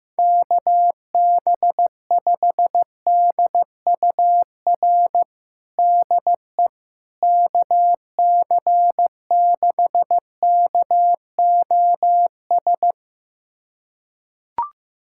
Morse code, heard as a keyed tone, about 15 wpm, 700 Hz. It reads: KB5DUR DE KC6KOS
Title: USA Callsign DE Callsign - 6 Characters - 15wpm